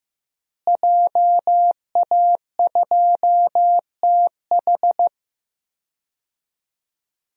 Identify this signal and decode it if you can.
Morse code, heard as a keyed tone, about 16 wpm, 700 Hz. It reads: JA2TH
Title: DXCC-100 Call Signs - 5 Characters - Repeated 3x - 15wpm